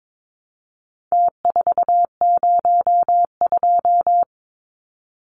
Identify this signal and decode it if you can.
Morse code, heard as a keyed tone, about 22 wpm, 700 Hz. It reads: T402
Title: Numbers - Contest Style - 4-Digit 22wpm